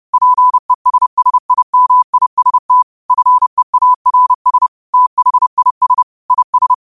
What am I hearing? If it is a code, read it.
Morse code, heard as a keyed tone, about 30 wpm, 1000 Hz. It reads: PESSIMIST FEARS THIS IS
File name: qotdfiles / 30wpm